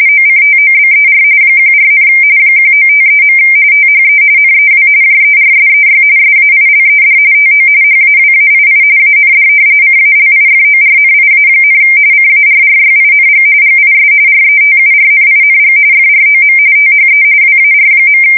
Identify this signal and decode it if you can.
RTTY, about 45.45 baud. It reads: OCTOBER 14, 1953. THE NEW JERSEY SAILED TO HAWAII AND CALIFORNIA, CARRYING GIS RETURNING FROM THE WAR.
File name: fsk170 / battleshipnj